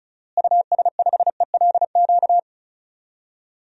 Morse code, heard as a keyed tone, about 34 wpm, 700 Hz. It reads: US5ELQ
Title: DXCC-100 Call Signs - 6 Characters - Repeated 2x - 35wpm